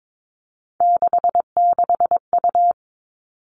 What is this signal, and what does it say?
Morse code, heard as a keyed tone, about 22 wpm, 700 Hz. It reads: 66U
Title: Three Challenging Letter-Number Combinations 22wpm